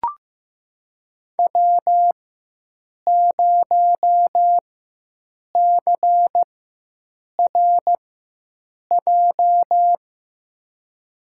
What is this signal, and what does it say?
Morse code, heard as a keyed tone, about 15 wpm, spaced as for 7 wpm, 700 Hz. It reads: W0CRJ
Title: USA Call Signs - 4x Character Spacing - 15wpm